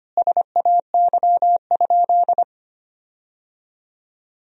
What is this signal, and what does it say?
Morse code, heard as a keyed tone, about 25 wpm, 700 Hz. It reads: SAY?
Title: 4x Word-Spacing - Sentences from Top 500 Words - 25wpm